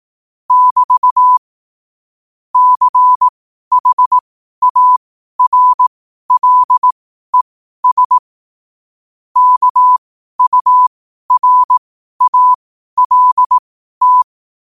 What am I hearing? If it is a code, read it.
Morse code, heard as a keyed tone, about 18 wpm, spaced as for 12 wpm, 1000 Hz. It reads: <BT> CHARLES KURALT
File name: qotdfiles / 12wpm